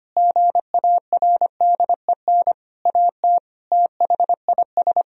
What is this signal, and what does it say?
Morse code, heard as a keyed tone, about 25 wpm, 700 Hz. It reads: GARDEN AT THIS
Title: Sentences from Top 1000 Words - Review Part 4 - 25wpm